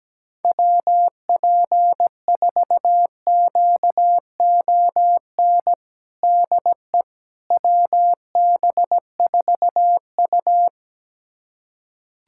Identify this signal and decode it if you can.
Morse code, heard as a keyed tone, about 17 wpm, 700 Hz. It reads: WP4QON DE WB4U
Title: USA Callsign DE Callsign 17wpm